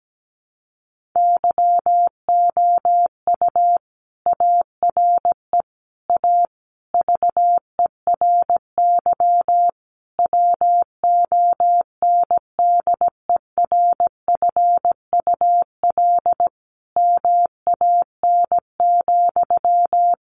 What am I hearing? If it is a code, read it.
Morse code, heard as a keyed tone, about 17 wpm, 700 Hz. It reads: YOU ARE A VERY WONDERFUL MAN,
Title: Sentences from Top 1100 Words 17wpm